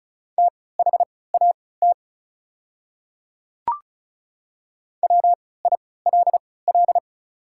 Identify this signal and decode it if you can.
Morse code, heard as a keyed tone, about 36 wpm, spaced as for 20 wpm, 690 Hz. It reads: THAT WILL
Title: 3x Character Spacing - Top 100 Words - 35wpm